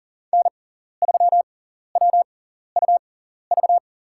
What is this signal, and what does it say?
Morse code, heard as a keyed tone, about 40 wpm, spaced as for 14 wpm, 700 Hz. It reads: N3WUV
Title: USA Call Signs - 5 Characters - 6x Character Spacing - 40wpm